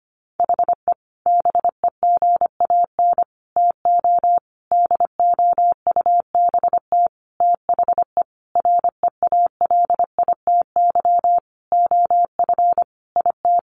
Morse code, heard as a keyed tone, about 25 wpm, 700 Hz. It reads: HE BEGAN TO DOUBT THE REALITY OF IT
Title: Sentences from Top 1500 Words 25wpm